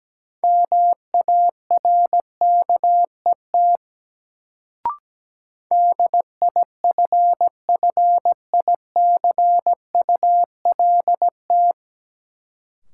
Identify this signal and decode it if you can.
Morse code, heard as a keyed tone, about 17 wpm, 700 Hz. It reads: MARKET DIFFICULT
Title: Sentences from Top 600 Words - New Word Drill - 17wpm